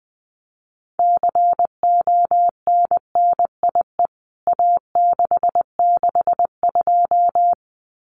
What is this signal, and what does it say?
Morse code, heard as a keyed tone, about 20 wpm, 700 Hz. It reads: CONNIE A662
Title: CWT Exchange 20wpm